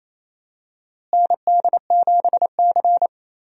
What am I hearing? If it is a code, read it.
Morse code, heard as a keyed tone, about 28 wpm, 700 Hz. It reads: ND7C
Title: USA Call Signs - 4 Characters - Repeated 3x - 28wpm